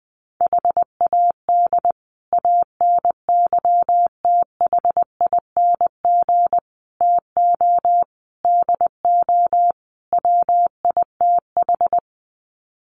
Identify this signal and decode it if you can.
Morse code, heard as a keyed tone, about 20 wpm, 700 Hz. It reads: HAD ANYTHING TO DO WITH
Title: Sets of 5 Words Encore 20wpm